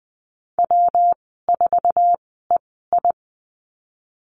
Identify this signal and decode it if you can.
Morse code, heard as a keyed tone, about 20 wpm, spaced as for 14 wpm, 700 Hz. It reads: W4EI
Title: 122 Lesson 20g - Number 3 - Call Signs 20wpm